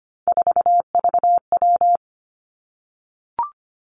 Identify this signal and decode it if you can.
Morse code, heard as a keyed tone, about 25 wpm, 700 Hz. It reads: 4VW
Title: Three Challenging Letter-Number Combinations 25wpm